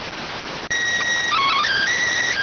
SSTV (slow-scan television), PD 290.